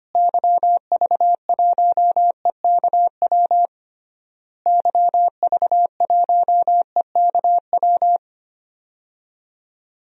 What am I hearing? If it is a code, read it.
Morse code, heard as a keyed tone, about 25 wpm, 700 Hz. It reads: YV1EKW YV1EKW
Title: DXCC-100 Call Signs - 6 Characters - Repeated 3x - 25wpm